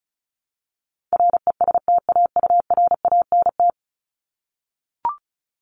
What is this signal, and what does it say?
Morse code, heard as a keyed tone, about 34 wpm, 700 Hz. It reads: RESTAURANT
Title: Top 1751-2000 Words 35wpm